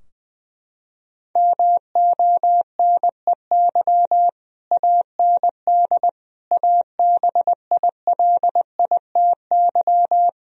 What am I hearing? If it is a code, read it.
Morse code, heard as a keyed tone, about 20 wpm, 700 Hz. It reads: MONEY AND ABILITY
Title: Sentences from Top 1000 Words - Review Part 3 - 20wpm